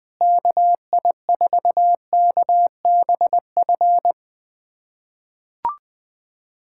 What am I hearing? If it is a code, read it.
Morse code, heard as a keyed tone, about 20 wpm, 700 Hz. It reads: KI4KBF
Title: USA Callsign DE Callsign - 6 Characters - 20wpm